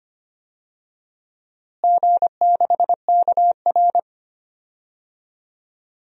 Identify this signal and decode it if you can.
Morse code, heard as a keyed tone, about 25 wpm, 700 Hz. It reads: G6KR
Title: DXCC-100 Call Signs - 4 Characters - Repeated 2x - 25wpm